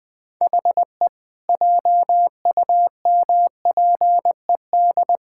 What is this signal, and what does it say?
Morse code, heard as a keyed tone, about 21 wpm, 700 Hz. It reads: HE JUMPED
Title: Sentences with 2-Word Phrases - 20wpm